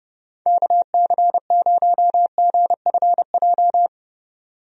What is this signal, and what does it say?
Morse code, heard as a keyed tone, about 30 wpm, 700 Hz. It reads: KC0GFJ
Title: USA Call Signs - 6 Characters - 30wpm